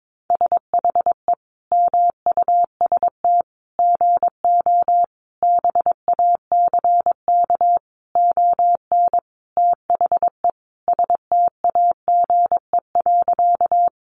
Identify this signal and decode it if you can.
Morse code, heard as a keyed tone, about 22 wpm, 700 Hz. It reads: SHE MUST GO BACK ON THE STAGE.
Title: Sentences from Top 1200 Words 22wpm